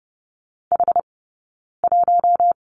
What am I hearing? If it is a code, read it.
Morse code, heard as a keyed tone, about 30 wpm, 700 Hz. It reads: H 1
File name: Lesson 18d - Letter m - Two Characters 30wpm